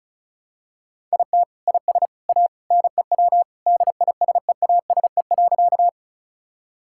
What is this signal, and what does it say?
Morse code, heard as a keyed tone, about 34 wpm, 690 Hz. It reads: IT IS A NEW DISEASE.
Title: Sentences from Top 1000 Words - Review Part 6 - 35wpm